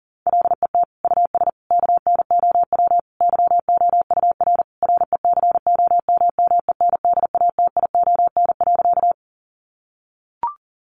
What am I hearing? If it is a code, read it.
Morse code, heard as a keyed tone, about 40 wpm, 700 Hz. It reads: LET US KNOW YOUR RECOMMENDATION.
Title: Sentences from Top 1400 Words 40wpm